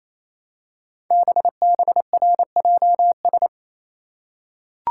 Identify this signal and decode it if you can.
Morse code, heard as a keyed tone, about 27 wpm, 700 Hz. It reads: BBRJS
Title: Code Groups - 5 Letters - 28wpm